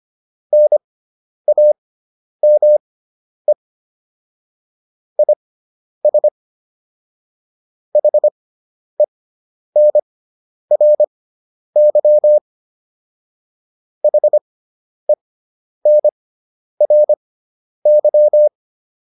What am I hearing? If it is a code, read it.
Morse code, heard as a keyed tone, about 25 wpm, spaced as for 10 wpm, 600 Hz. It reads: NAME IS HENRY HENRY
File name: QSO102 10